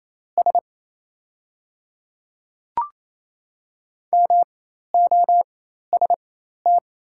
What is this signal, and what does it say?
Morse code, heard as a keyed tone, about 28 wpm, spaced as for 13 wpm, 690 Hz. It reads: S MOST
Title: 4x Character Spacing - Top 500 Words - 28wpm